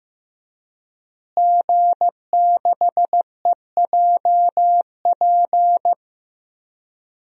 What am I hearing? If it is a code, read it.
Morse code, heard as a keyed tone, about 15 wpm, 700 Hz. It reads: G6EJP
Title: DXCC-100 Call Signs - 5 Characters - Repeated 2x - 15wpm